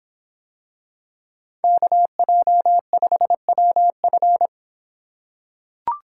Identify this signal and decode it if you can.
Morse code, heard as a keyed tone, about 26 wpm, 700 Hz. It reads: KJ5WF